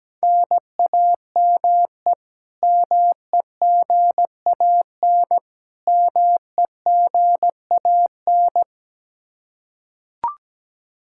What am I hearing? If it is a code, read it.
Morse code, heard as a keyed tone, about 17 wpm, 700 Hz. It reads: NAME MEGAN MEGAN
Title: Practice QSO - Exchange 1 - Beginning - 17wpm